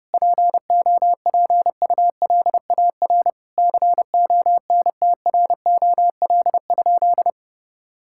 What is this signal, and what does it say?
Morse code, heard as a keyed tone, about 32 wpm, 700 Hz. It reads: POPULAR CONTROL?
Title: Sentences from Top 1000 Words - Review Part 5 - 30wpm